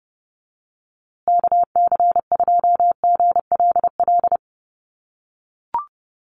Morse code, heard as a keyed tone, about 30 wpm, 700 Hz. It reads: KC2GLL
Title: USA Call Signs Repeated 3x 30wpm